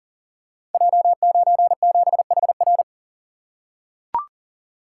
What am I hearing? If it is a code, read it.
Morse code, heard as a keyed tone, about 41 wpm, 700 Hz. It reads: J97HR
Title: Code Groups - 5 Letter:Numbers - 40wpm